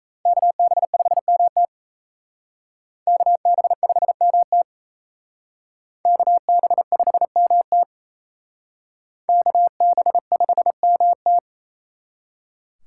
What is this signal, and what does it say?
Morse code, heard as a keyed tone, about 34 wpm, 700 Hz. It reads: KB5MT KB5MT KB5MT KB5MT
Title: Speed Racing - USA 2x2 Call Signs 28wpm